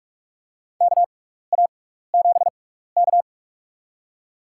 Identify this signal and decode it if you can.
Morse code, heard as a keyed tone, about 46 wpm, spaced as for 16 wpm, 700 Hz. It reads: KA7K
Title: USA Call Signs - 4 Characters - 6x Character Spacing - 45wpm